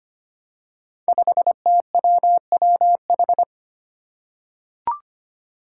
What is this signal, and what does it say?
Morse code, heard as a keyed tone, about 25 wpm, 700 Hz. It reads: 5TWWH
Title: Five Challenging Letter-Number Combinations 25wpm